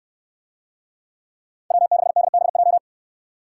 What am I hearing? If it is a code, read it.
Morse code, heard as a keyed tone, about 69 wpm, 700 Hz. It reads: W6DBJ